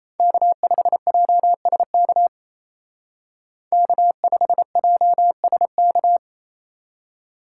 Morse code, heard as a keyed tone, about 34 wpm, 700 Hz. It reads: K5JSK K5JSK
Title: Speed Racing - USA 1x3 Call Signs 28wpm